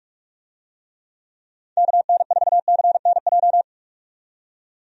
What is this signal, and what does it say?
Morse code, heard as a keyed tone, about 46 wpm, 700 Hz. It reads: KN4CNJ
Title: USA Call Signs - 6 Characters - 45wpm